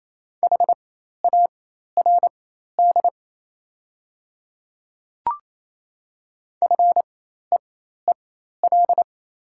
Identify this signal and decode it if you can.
Morse code, heard as a keyed tone, about 28 wpm, spaced as for 13 wpm, 700 Hz. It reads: HARD FEEL
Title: 4x Character Spacing - Top 500 Words - 28wpm